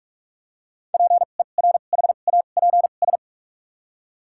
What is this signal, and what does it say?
Morse code, heard as a keyed tone, about 42 wpm, spaced as for 32 wpm, 700 Hz. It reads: PERHAPS
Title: 2x Character Spacing - Top 500 Words - 45wpm